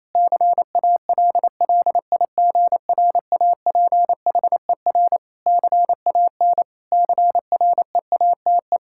Morse code, heard as a keyed tone, about 27 wpm, 700 Hz. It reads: CALLIGRAPHER CAN CREATE
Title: Pangram Challenge v2 - 28wpm